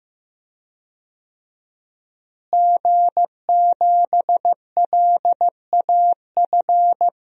Morse code, heard as a keyed tone, about 15 wpm, 700 Hz. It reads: G7LAF
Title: DXCC-100 Call Signs - 5 Characters - Repeated 2x - 15wpm